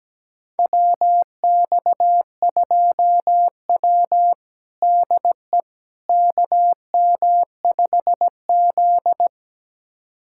Morse code, heard as a keyed tone, about 17 wpm, 700 Hz. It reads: WX2W DE KM5Z
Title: USA Callsign DE Callsign - 4 Characters - 17wpm